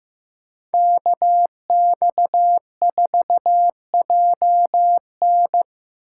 Morse code, heard as a keyed tone, about 16 wpm, 700 Hz. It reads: KX4JN
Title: USA Call Signs - 5 Characters - 8x Character Spacing - 15wpm